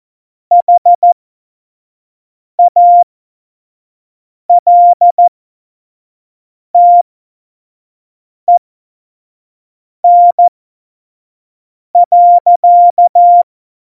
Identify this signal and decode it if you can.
Morse code, heard as a keyed tone, about 14 wpm, spaced as for 5 wpm, 700 Hz. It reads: HALTEN.
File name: Morsekurs / Witze / Witz 08. Dezember2020, 25 F70